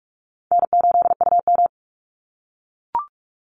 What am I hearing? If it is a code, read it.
Morse code, heard as a keyed tone, about 44 wpm, 700 Hz. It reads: N8UM
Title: Easy Call Signs - 45wpm